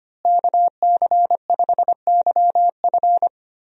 Morse code, heard as a keyed tone, about 25 wpm, 700 Hz. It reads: KC5YF
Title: USA Call Signs - 5 Characters - Repeated 2x - 25wpm